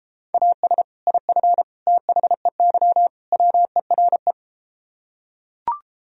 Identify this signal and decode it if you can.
Morse code, heard as a keyed tone, about 33 wpm, 700 Hz. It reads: AS IF THEY WERE